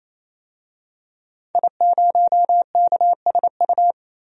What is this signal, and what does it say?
Morse code, heard as a keyed tone, about 28 wpm, 700 Hz. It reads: I0KSU
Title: DXCC-100 Call Signs - 5 Characters - 28wpm